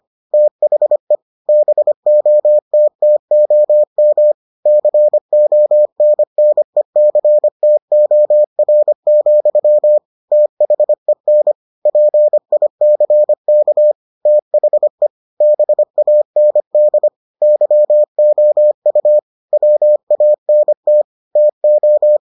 Morse code, heard as a keyed tone, about 25 wpm, 600 Hz. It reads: THE BOTTOM CONNECTOR, THEN PICK THE BAND YOU WANT TO